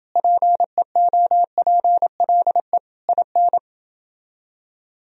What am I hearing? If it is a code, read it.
Morse code, heard as a keyed tone, about 31 wpm, 700 Hz. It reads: PEOPLE IN